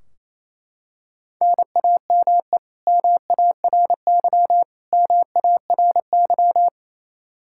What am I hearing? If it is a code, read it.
Morse code, heard as a keyed tone, about 28 wpm, 700 Hz. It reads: NAME MARY MARY
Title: K1USN SST QSO Practice 28wpm